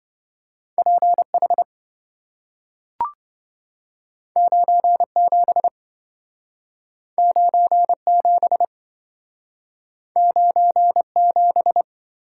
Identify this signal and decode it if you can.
Morse code, heard as a keyed tone, about 27 wpm, 700 Hz. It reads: PH 97 97 97